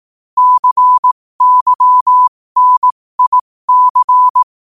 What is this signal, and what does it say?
Morse code, heard as a keyed tone, about 18 wpm, spaced as for 15 wpm, 1000 Hz. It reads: CYNIC
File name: qotdfiles / 15wpm